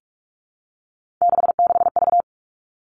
Morse code, heard as a keyed tone, about 46 wpm, 700 Hz. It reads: 66V
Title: Three Challenging Letter-Number Combinations 45wpm